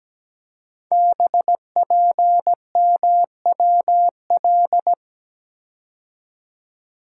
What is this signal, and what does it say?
Morse code, heard as a keyed tone, about 17 wpm, 700 Hz. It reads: BPMWL
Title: Code Groups - 5 Letters - 17wpm